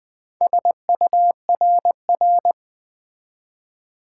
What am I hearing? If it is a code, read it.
Morse code, heard as a keyed tone, about 20 wpm, 700 Hz. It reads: SURR
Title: Build A Word - 4 Letters At A Time - 11-Letter Words - 20wpm